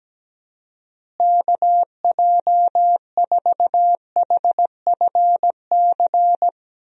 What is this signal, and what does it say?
Morse code, heard as a keyed tone, about 17 wpm, 700 Hz. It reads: KJ4HFC
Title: USA Call Signs - 6 Characters - Repeated 3x - 17wpm